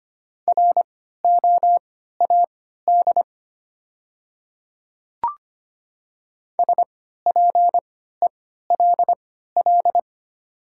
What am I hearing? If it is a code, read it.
Morse code, heard as a keyed tone, about 25 wpm, spaced as for 14 wpm, 690 Hz. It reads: ROAD SPELL
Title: 3x Character Spacing - Top 500 Words - 25wpm